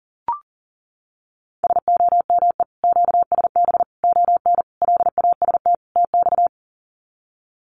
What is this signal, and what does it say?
Morse code, heard as a keyed tone, about 40 wpm, 700 Hz. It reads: SOME QSB ON LAST TX
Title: Common QSO Phrases 40wpm